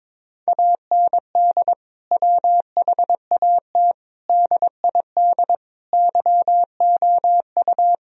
Morse code, heard as a keyed tone, about 22 wpm, 700 Hz. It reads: AND WHAT DID YOU
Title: Sentences from Top 1000 Words - Review Part 3 - 22wpm